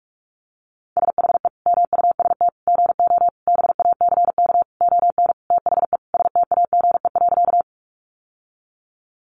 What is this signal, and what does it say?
Morse code, heard as a keyed tone, about 44 wpm, 700 Hz. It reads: SHE MUST GO BACK ON THE STAGE.
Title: Sentences from Top 1200 Words 45wpm